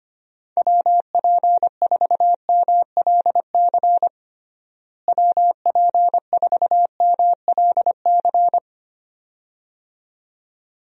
Morse code, heard as a keyed tone, about 25 wpm, 700 Hz. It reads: WP4MLC WP4MLC
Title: USA Call Signs - 6 Characters - Repeated 2x - 25wpm